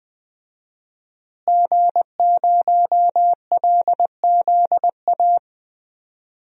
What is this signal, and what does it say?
Morse code, heard as a keyed tone, about 20 wpm, 700 Hz. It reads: G0LZA
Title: DXCC-100 Call Signs - 5 Characters - Repeated 2x - 20wpm